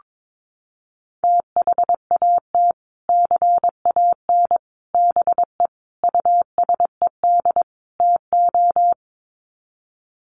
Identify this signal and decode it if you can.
Morse code, heard as a keyed tone, about 22 wpm, 700 Hz. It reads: THAT CAN BE USED TO
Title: Sets of 5 Words Encore 22wpm